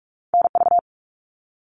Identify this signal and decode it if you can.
Morse code, heard as a keyed tone, about 46 wpm, 700 Hz. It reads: NV
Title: ARRL Sections 45wpm